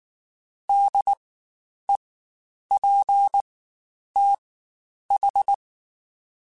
Morse code, heard as a keyed tone, about 19 wpm, spaced as for 9 wpm, 800 Hz. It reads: DEPTH
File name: Spaced Alphas / Spaced Alphas mp3s / Ds sp @18